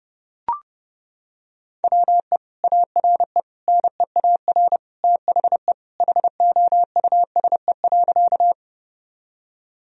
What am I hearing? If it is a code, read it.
Morse code, heard as a keyed tone, about 30 wpm, 700 Hz. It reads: WE ARE NEAR THE HOUSE.
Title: Sentences from Top 1000 Words - Review Part 1 - 30wpm